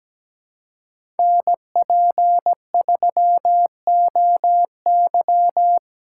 Morse code, heard as a keyed tone, about 17 wpm, 700 Hz. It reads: NP3OY
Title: USA Call Signs - 5 Characters - 4x Character Spacing - 17wpm